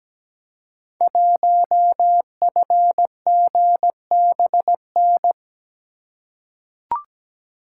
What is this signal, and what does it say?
Morse code, heard as a keyed tone, about 17 wpm, 700 Hz. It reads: 1FGBN
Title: Code Groups - 5 Letter:Numbers - 17wpm